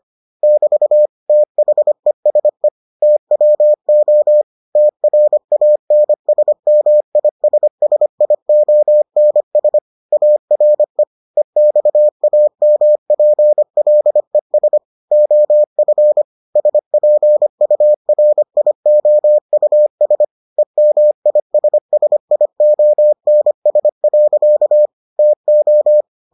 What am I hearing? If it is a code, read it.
Morse code, heard as a keyed tone, about 25 wpm, 600 Hz. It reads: <BT> THESE TWO TRANSMISSIONS ARE EXAMPLES OF SPURIOUS EMISSIONS. TO